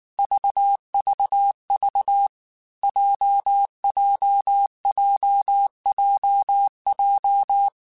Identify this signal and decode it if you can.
Morse code, heard as a keyed tone, about 19 wpm, 800 Hz. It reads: VVV JJJJJ
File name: Basic Sets / Basic Letter Groups Collective7 / JZ17@18